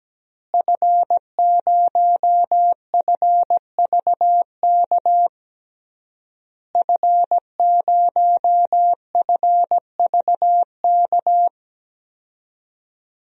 Morse code, heard as a keyed tone, about 17 wpm, 700 Hz. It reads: F0FVK F0FVK
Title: DXCC-100 Call Signs - 5 Characters - Repeated 3x - 17wpm